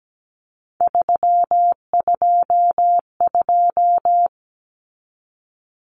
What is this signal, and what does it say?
Morse code, heard as a keyed tone, about 17 wpm, 700 Hz. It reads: 322
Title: Numbers - 3-Digit 17wpm